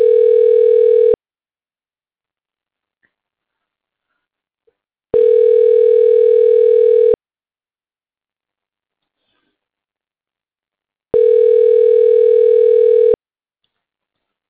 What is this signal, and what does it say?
Ringback tone (North America), 440 + 480 Hz.